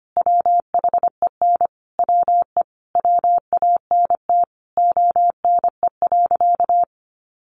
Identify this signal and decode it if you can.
Morse code, heard as a keyed tone, about 25 wpm, 700 Hz. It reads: WHEN WE WANT ONE.
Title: Sentences from Top 1200 Words 25wpm